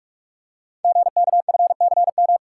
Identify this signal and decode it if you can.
Morse code, heard as a keyed tone, about 44 wpm, 700 Hz. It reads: GKFCM
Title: Code Groups - 5 Letters - 45wpm